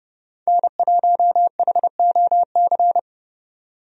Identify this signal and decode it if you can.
Morse code, heard as a keyed tone, about 30 wpm, 700 Hz. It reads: N1HOC
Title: USA Call Signs - 5 Characters - 30wpm